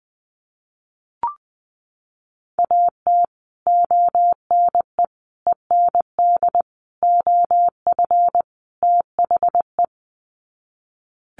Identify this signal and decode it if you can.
Morse code, heard as a keyed tone, about 20 wpm, 700 Hz. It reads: AT ONE END OF THE
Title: Sets of 5 Words Encore 20wpm